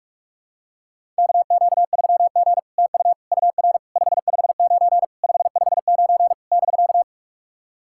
Morse code, heard as a keyed tone, about 44 wpm, 700 Hz. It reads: KQ3G TU UR 559 559 <BK>
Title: SOTA QSO Practice 45wpm